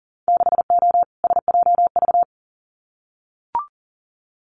Morse code, heard as a keyed tone, about 40 wpm, 700 Hz. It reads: 6O SJV
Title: Winter Field Day Exchanges Encore 40wpm